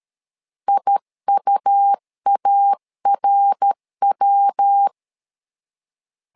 Morse code, heard as a keyed tone, about 13 wpm, 800 Hz. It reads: IUARW
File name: LEZIONI / 3 MWRO / 3 MWR 12wpm 1711